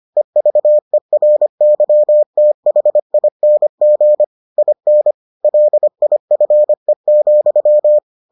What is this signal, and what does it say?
Morse code, heard as a keyed tone, about 25 wpm, 600 Hz. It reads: EVERYTHING IN LIFE,